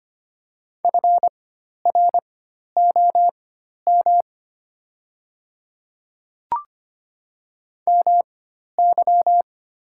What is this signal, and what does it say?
Morse code, heard as a keyed tone, about 25 wpm, spaced as for 12 wpm, 700 Hz. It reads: FROM MY
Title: 4x Character Spacing - Top 100 Words - 25wpm